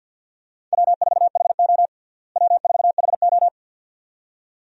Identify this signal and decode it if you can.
Morse code, heard as a keyed tone, about 52 wpm, 700 Hz. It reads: W4HO W4HO
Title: USA Call Signs - 4 Characters - Repeated 2x - 50wpm